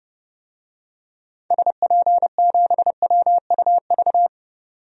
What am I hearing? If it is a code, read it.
Morse code, heard as a keyed tone, about 31 wpm, 700 Hz. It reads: SP7WUV
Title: DXCC-100 Call Signs - 6 Characters - 30wpm